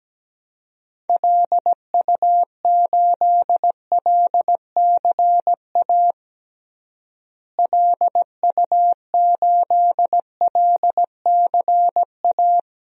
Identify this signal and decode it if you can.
Morse code, heard as a keyed tone, about 17 wpm, 700 Hz. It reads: LU8LCA LU8LCA
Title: DXCC-100 Call Signs - 6 Characters - Repeated 2x - 17wpm